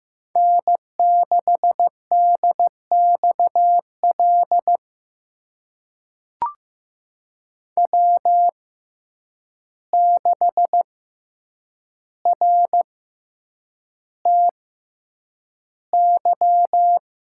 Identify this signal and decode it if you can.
Morse code, heard as a keyed tone, about 15 wpm, spaced as for 8 wpm, 700 Hz. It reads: N6DXL W6RTY
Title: USA Call Signs - 5 Characters - 6x Character Spacing - 15wpm